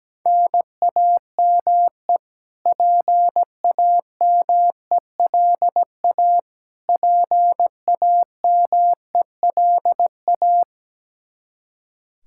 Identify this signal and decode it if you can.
Morse code, heard as a keyed tone, about 17 wpm, 700 Hz. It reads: NAME PAMELA PAMELA
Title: Practice QSO - Exchange 1 - Beginning - 17wpm